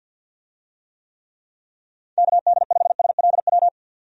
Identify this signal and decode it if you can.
Morse code, heard as a keyed tone, about 50 wpm, 700 Hz. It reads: KD5SLW
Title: USA Call Signs - 6 Characters - 50wpm